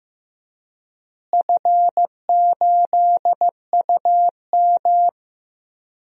Morse code, heard as a keyed tone, about 15 wpm, 700 Hz. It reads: F8UM
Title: DXCC-100 Call Signs - 4 Characters - Repeated 2x - 15wpm